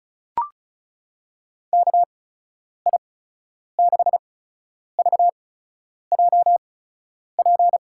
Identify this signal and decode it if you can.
Morse code, heard as a keyed tone, about 36 wpm, spaced as for 10 wpm, 700 Hz. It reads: KI6VJP
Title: USA Call Signs - 6 Characters - 8x Character Spacing - 35wpm